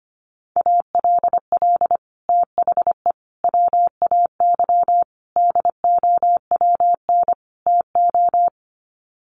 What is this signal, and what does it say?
Morse code, heard as a keyed tone, about 25 wpm, 700 Hz. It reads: ALL THE WAY DOWN TO
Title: Sets of 5 Words Encore2 25wpm